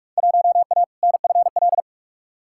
Morse code, heard as a keyed tone, about 43 wpm, 700 Hz. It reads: 1A NFL
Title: Field Day Exchanges - 45wpm